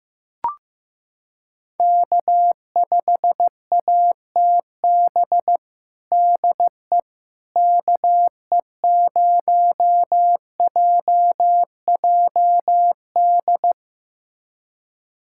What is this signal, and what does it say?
Morse code, heard as a keyed tone, about 15 wpm, 700 Hz. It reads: K5ATB DE KE0JJD
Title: USA Callsign DE Callsign 15wpm